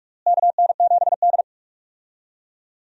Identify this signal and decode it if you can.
Morse code, heard as a keyed tone, about 45 wpm, 700 Hz. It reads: KN7D
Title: USA Call Signs - 4 Characters - Repeated 2x - 45wpm